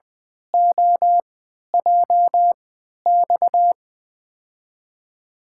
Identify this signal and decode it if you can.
Morse code, heard as a keyed tone, about 20 wpm, spaced as for 11 wpm, 700 Hz. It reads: OJX
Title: USA Call Signs - 6 Characters - 3x Character Spacing - 20wpm